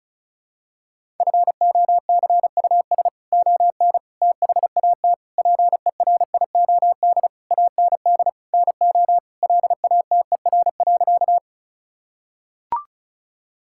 Morse code, heard as a keyed tone, about 36 wpm, 700 Hz. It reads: FOCUS ON THAT PERIOD AND NO LATER.
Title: Sentences from Top 1000 Words - Review Part 5 - 35wpm